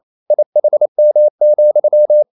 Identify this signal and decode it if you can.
Morse code, heard as a keyed tone, about 28 wpm, 600 Hz. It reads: IHM,
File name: Liebespaar4 28wpm